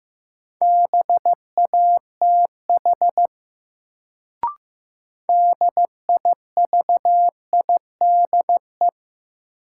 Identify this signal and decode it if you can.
Morse code, heard as a keyed tone, about 15 wpm, 700 Hz. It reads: BATH DIVIDE
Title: Sentences from Top 1600 Words - New Word Drill - 15wpm